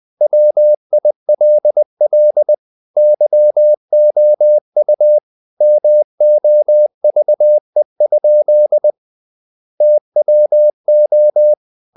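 Morse code, heard as a keyed tone, about 20 wpm, 600 Hz. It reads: WILL YOU MOVE? TWO
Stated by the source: Morse code at 19 wpm from CW med Gnister, reading Ken Follett's novel Never in English.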